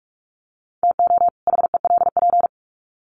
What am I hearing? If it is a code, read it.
Morse code, heard as a keyed tone, about 44 wpm, 700 Hz. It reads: TO HELP
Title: Sets of 2 Words 45wpm